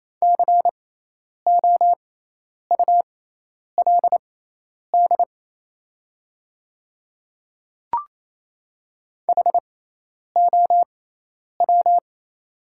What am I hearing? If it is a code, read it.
Morse code, heard as a keyed tone, about 28 wpm, spaced as for 10 wpm, 700 Hz. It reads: COULD HOW
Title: 6x Character Spacing - Top 200 Words - 28wpm